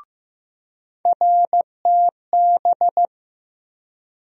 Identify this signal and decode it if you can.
Morse code, heard as a keyed tone, about 15 wpm, 700 Hz. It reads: RTB
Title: USA Call Sign Postfixes - 3 Letters 15wpm